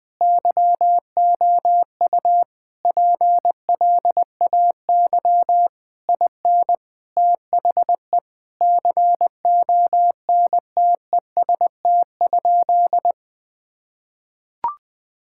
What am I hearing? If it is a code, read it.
Morse code, heard as a keyed tone, about 21 wpm, 700 Hz. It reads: YOU PLAY IN THE CONTEST?
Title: Sentences from Top 1600 Words 20wpm